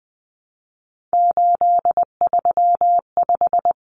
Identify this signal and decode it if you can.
Morse code, heard as a keyed tone, about 20 wpm, 700 Hz. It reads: 835
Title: Numbers - 3-Digit 20wpm